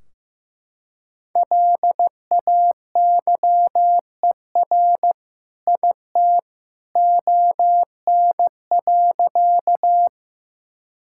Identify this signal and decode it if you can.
Morse code, heard as a keyed tone, about 15 wpm, 700 Hz. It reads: LAYER IT ON.
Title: Sentences from Top 1600 Words 15wpm